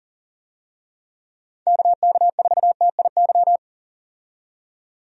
Morse code, heard as a keyed tone, about 40 wpm, 700 Hz. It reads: KK4TIY
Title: USA Call Signs - 6 Characters - 40wpm